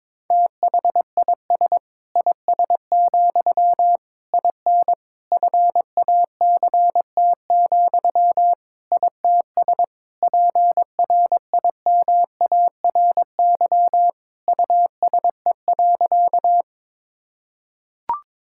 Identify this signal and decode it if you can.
Morse code, heard as a keyed tone, about 22 wpm, 700 Hz. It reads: THIS IS, IN FACT, ITS PRIMARY USE.
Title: Sentences from Top 1000 Words - Review Part 5 - 22wpm